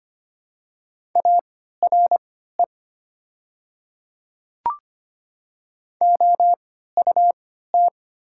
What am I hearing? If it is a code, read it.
Morse code, heard as a keyed tone, about 25 wpm, spaced as for 14 wpm, 700 Hz. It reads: ARE OUT
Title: 3x Character Spacing - Top 100 Words - 25wpm